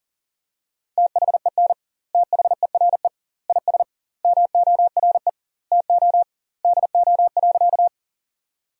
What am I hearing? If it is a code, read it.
Morse code, heard as a keyed tone, about 40 wpm, 700 Hz. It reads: THEN THERE IS MORE TO DO.
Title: 2x Word-Spacing - Sentences from Top 100 Words - 40wpm